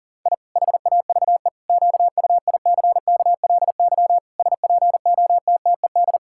Morse code, heard as a keyed tone, about 40 wpm, 700 Hz. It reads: I HAVE QUICKLY SPOTTED
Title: Pangram Challenge v2 - 40wpm